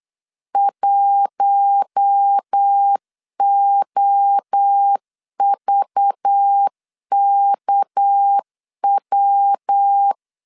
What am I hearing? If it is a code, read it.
Morse code, heard as a keyed tone, about 8 wpm, 800 Hz. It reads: 1OVKW
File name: LEZIONI / 6 PKX0 / 6 PKX 8wpm 3041